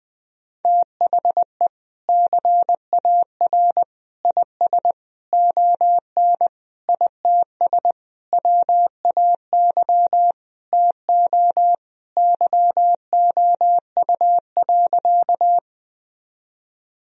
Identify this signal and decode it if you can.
Morse code, heard as a keyed tone, about 20 wpm, 700 Hz. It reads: THE CAR IS ON ITS WAY TO YOU.
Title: Sentences from Top 1000 Words - Review Part 1 - 20wpm